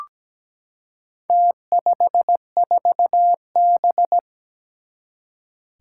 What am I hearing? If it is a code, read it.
Morse code, heard as a keyed tone, about 17 wpm, 700 Hz. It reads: T54B
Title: Four Challenging Letter-Number Combinations 17wpm